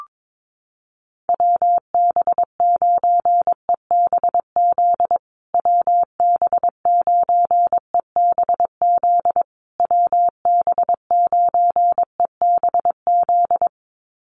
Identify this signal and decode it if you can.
Morse code, heard as a keyed tone, about 22 wpm, 700 Hz. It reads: WB9EBZ WB9EBZ WB9EBZ
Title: USA Call Signs Repeated 3x 22wpm